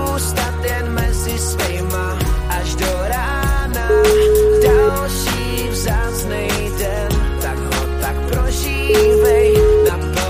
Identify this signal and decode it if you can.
Ringback tone (Europe, 425 Hz), with noise.